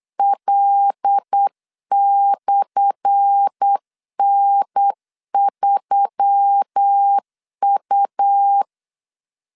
Morse code, heard as a keyed tone, about 8 wpm, 800 Hz. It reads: L/N3U
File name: LEZIONI / 10 SIGNS / 10A SLASH 8wpm 3452